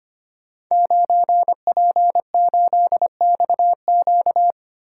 Morse code, heard as a keyed tone, about 25 wpm, 700 Hz. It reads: 9P8XQ
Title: Code Groups - 5 Letter:Numbers - 25wpm